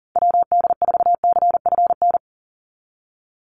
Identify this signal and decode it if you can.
Morse code, heard as a keyed tone, about 40 wpm, 700 Hz. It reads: WD4CFN
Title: NA SOTA Chaser Callsigns 40wpm